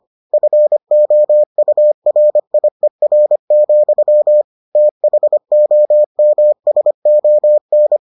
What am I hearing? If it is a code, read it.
Morse code, heard as a keyed tone, about 25 wpm, 600 Hz. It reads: FOURIER, THOMSON